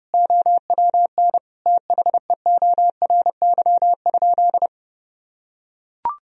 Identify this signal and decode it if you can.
Morse code, heard as a keyed tone, about 30 wpm, 700 Hz. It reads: OWN THEORY?
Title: Sentences from Top 1000 Words - Review Part 3 - 30wpm